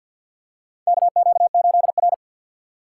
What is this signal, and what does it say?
Morse code, heard as a keyed tone, about 51 wpm, 690 Hz. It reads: KQ8R
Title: USA Call Signs - 2x Character Spacing - 50wpm